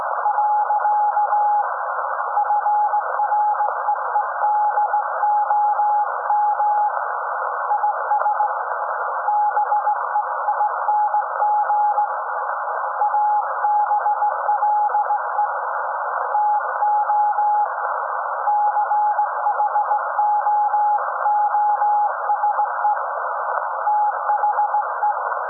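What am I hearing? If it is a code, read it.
Morse code, heard as a keyed tone, about 18 wpm, 840 Hz, with noise. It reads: . FREEDOM IS BEING ABLE TO CHOOSE THE
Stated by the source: SAQ Grimeton Radio, Alexanderson 200 kW alternator on 17.2 kHz, received in Potsdam with a very strong signal on World Radio Day.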